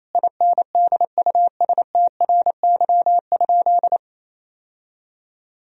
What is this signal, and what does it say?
Morse code, heard as a keyed tone, about 28 wpm, 700 Hz. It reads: INDUSTRY?
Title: Sentences from Top 1000 Words - Review Part 4 - 28wpm